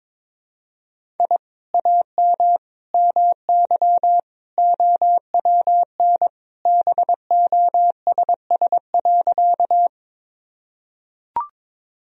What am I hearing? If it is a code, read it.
Morse code, heard as a keyed tone, about 22 wpm, 700 Hz. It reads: I AM MY OWN BOSS.
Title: Sentences from Top 1000 Words - Review Part 4 - 22wpm